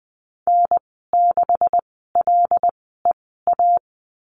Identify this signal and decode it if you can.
Morse code, heard as a keyed tone, about 20 wpm, spaced as for 14 wpm, 700 Hz. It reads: N6LEA
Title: 186 Lesson 30g - Letter b - Call Signs 20wpm